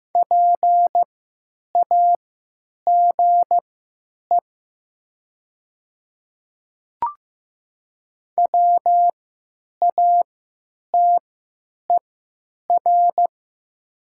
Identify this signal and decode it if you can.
Morse code, heard as a keyed tone, about 15 wpm, spaced as for 9 wpm, 700 Hz. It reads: PAGE WATER
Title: 3x Character Spacing - Top 300 Words - 15wpm